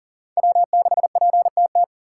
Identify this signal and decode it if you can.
Morse code, heard as a keyed tone, about 40 wpm, 700 Hz. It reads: W6PTT
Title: USA Call Signs - 5 Characters - 3x Character Spacing - 40wpm